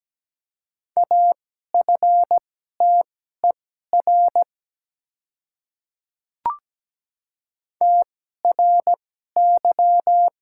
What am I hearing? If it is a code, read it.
Morse code, heard as a keyed tone, about 17 wpm, spaced as for 12 wpm, 700 Hz. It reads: AFTER TRY
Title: 2x Character Spacing - Top 200 Words - 17wpm